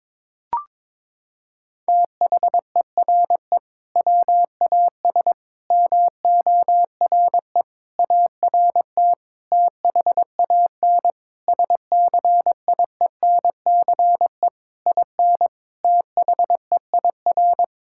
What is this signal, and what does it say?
Morse code, heard as a keyed tone, about 22 wpm, 700 Hz. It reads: THERE WAS MORE ART THAN SCIENCE IN THEIR
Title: Sentences from Top 1000 Words - Review Part 6 - 22wpm